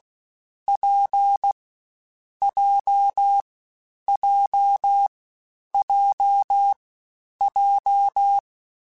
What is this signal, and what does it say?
Morse code, heard as a keyed tone, about 16 wpm, 800 Hz. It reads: P J J J J
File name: Difficult Combos / Difficult Combos-mp3s / WPJ WPJ1 @15